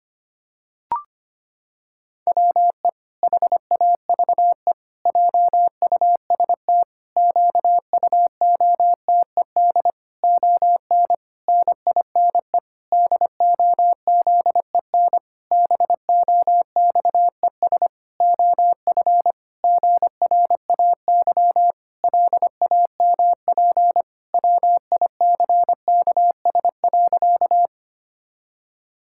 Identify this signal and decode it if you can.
Morse code, heard as a keyed tone, about 25 wpm, 700 Hz. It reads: WE HAVE JUST QUOTED ON NINE DOZEN BOXES OF GRAY LAMP WICKS.
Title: Pangram Challenge v2 - 25wpm